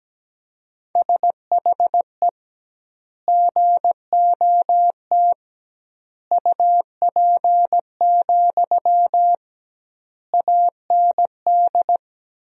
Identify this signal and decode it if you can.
Morse code, heard as a keyed tone, about 17 wpm, 700 Hz. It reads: SHE GOT UP, AND
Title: 2x Word-Spacing - Sentences from Top 500 Words - 17wpm